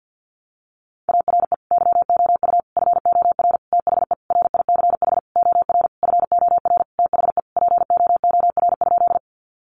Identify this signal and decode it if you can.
Morse code, heard as a keyed tone, about 50 wpm, 700 Hz. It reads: ARE YOU FOR THE RICH OR FOR THE POOR?
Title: Sentences from Top 1100 Words 50wpm